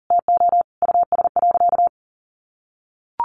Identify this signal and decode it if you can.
Morse code, heard as a keyed tone, about 37 wpm, 700 Hz. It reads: TO US.
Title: Sentences from Top 800 Words 40wpm